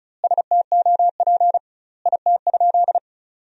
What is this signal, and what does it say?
Morse code, heard as a keyed tone, about 36 wpm, 700 Hz. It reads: STOP IT?
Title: 2x Word-Spacing - Sentences from Top 300 Words - 35wpm